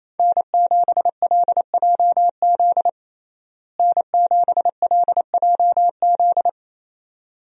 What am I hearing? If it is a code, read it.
Morse code, heard as a keyed tone, about 28 wpm, 700 Hz. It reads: N7LJZ N7LJZ
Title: USA Call Signs - 5 Characters - Repeated 2x - 28wpm